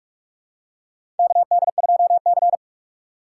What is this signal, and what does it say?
Morse code, heard as a keyed tone, about 43 wpm, 700 Hz. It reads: KD2C
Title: USA Call Signs - 4 Characters - 4x Character Spacing - 45wpm